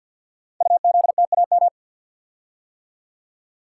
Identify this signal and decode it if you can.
Morse code, heard as a keyed tone, about 48 wpm, 700 Hz. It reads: UZTAM
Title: Code Groups - 5 Letters - 50wpm